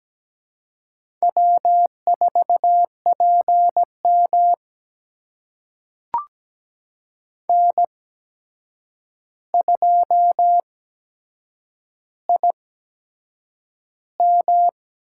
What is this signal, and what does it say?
Morse code, heard as a keyed tone, about 17 wpm, spaced as for 7 wpm, 700 Hz. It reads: W4PM N2IM
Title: USA Call Signs - 4 Characters - 8x Character Spacing - 17wpm